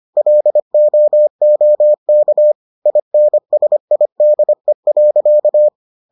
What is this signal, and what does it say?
Morse code, heard as a keyed tone, about 25 wpm, 600 Hz. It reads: LOOK INSIDE.